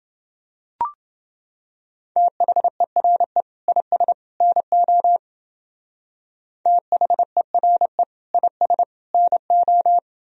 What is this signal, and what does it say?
Morse code, heard as a keyed tone, about 28 wpm, 700 Hz. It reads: THERE IS NO THERE IS NO